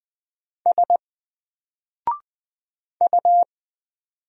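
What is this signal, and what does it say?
Morse code, heard as a keyed tone, about 20 wpm, 700 Hz. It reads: S U
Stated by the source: Single letter and number Morse drill at 20 wpm, from N4LCD.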